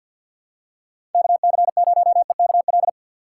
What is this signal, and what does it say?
Morse code, heard as a keyed tone, about 51 wpm, 700 Hz. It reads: KC0EKL
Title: USA Call Signs - 4x Character Spacing - 50wpm